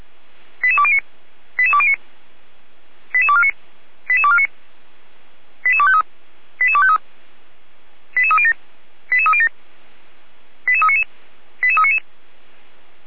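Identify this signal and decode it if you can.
Five-tone selective call (ZVEI), heard as quick runs of five tones, tones of about 70 ms. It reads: 88209 88209 88249 88249 88253 88253 88307 88307 88399 88399
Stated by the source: ZVEI five-tone call received by SDR.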